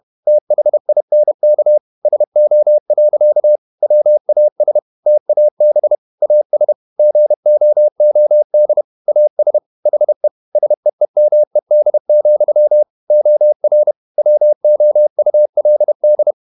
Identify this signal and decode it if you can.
Morse code, heard as a keyed tone, about 31 wpm, 600 Hz. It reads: THINK SO. WAS TAB AS GOOD AS HE SEEMED, OR WOULD
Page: Never 31wpm | CW med Gnister